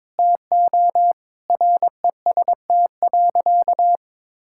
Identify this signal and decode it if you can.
Morse code, heard as a keyed tone, about 21 wpm, 700 Hz. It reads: TO REST.
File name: practice / 22wpm / Sentences with 8-Word Phrases - 22wpm